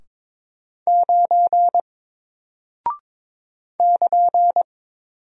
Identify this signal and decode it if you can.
Morse code, heard as a keyed tone, about 22 wpm, 700 Hz. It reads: 9 <KN>
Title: Single Character (Letter:Number:Punctuation:Prosign) 22wpm